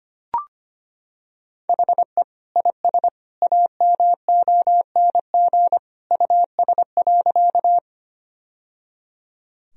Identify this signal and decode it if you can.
Morse code, heard as a keyed tone, about 25 wpm, 700 Hz. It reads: HE IS AMONG US.
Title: Sentences from Top 1000 Words - Review Part 3 - 25wpm